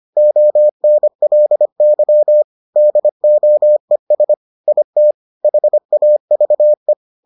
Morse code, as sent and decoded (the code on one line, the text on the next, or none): --- -. .-.. -.-- / -.. --- . ... / .. - / .... .- ...- .
ONLY DOES IT HAVE